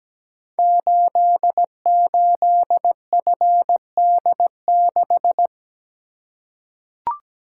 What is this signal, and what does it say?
Morse code, heard as a keyed tone, about 17 wpm, 700 Hz. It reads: 88FD6
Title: Code Groups - 5 Letter:Numbers - 17wpm